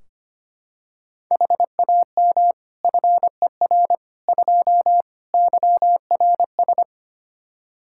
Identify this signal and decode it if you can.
Morse code, heard as a keyed tone, about 25 wpm, 700 Hz. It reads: HAM FER 2 YRS
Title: Practice QSO - Exchange 3 - More Info - 25wpm